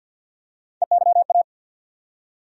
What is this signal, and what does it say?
Morse code, heard as a keyed tone, about 52 wpm, 690 Hz. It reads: EYA
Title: USA Call Sign Postfixes - 3 Letters 50wpm